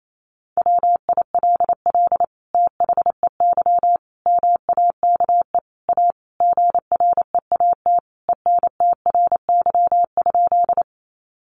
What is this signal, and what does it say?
Morse code, heard as a keyed tone, about 28 wpm, 700 Hz. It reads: WILL THEY MAKE A GREAT ENTRY?
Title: Sentences from Top 1500 Words 28wpm